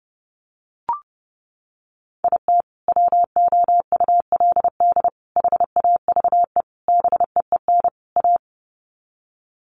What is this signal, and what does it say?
Morse code, heard as a keyed tone, about 30 wpm, 700 Hz. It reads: IT WOULD HAVE BEEN A
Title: Sets of 5 Words Encore 30wpm